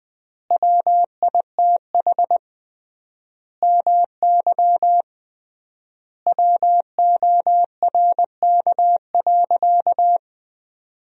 Morse code, heard as a keyed tone, about 20 wpm, 700 Hz. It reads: WITH MY WORK.
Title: 3x Word-Spacing - Sentences from Top 300 Words - 20wpm